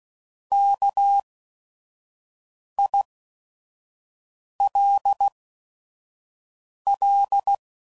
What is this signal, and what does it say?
Morse code, heard as a keyed tone, about 16 wpm, spaced as for 5 wpm, 800 Hz. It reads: KILL